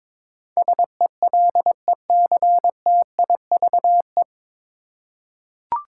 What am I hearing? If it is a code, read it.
Morse code, heard as a keyed tone, about 22 wpm, 700 Hz. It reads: SELECTIVE
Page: Morse Code Ninja practice 22wpm